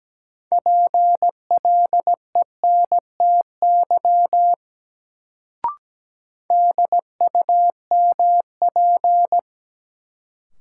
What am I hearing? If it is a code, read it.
Morse code, heard as a keyed tone, about 17 wpm, 700 Hz. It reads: PLENTY DUMP
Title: Sentences from Top 1300 Words - New Word Drill - 17wpm